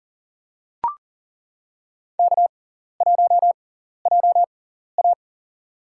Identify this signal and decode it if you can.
Morse code, heard as a keyed tone, about 40 wpm, spaced as for 14 wpm, 700 Hz. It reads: K1JA
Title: USA Call Signs - 4 Characters - 6x Character Spacing - 40wpm